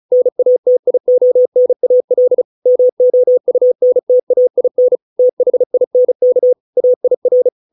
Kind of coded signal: Morse code